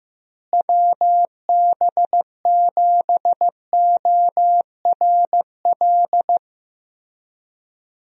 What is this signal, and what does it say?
Morse code, heard as a keyed tone, about 15 wpm, 700 Hz. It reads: WB7ORL
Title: USA Call Signs - 6 Characters - Repeated 3x - 15wpm